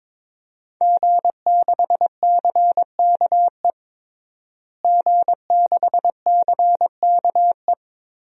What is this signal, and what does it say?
Morse code, heard as a keyed tone, about 22 wpm, 700 Hz. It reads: G6CKE G6CKE
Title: DXCC-100 Call Signs - 5 Characters - Repeated 2x - 22wpm